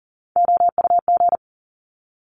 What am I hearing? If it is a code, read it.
Morse code, heard as a keyed tone, about 40 wpm, 700 Hz. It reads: OUG
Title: Top Three Letter Combinations in Words 40wpm